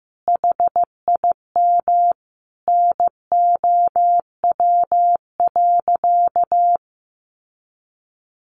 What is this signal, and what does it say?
Morse code, heard as a keyed tone, about 15 wpm, 700 Hz. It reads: HIM NOW.
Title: Sentences from Top 1100 Words 15wpm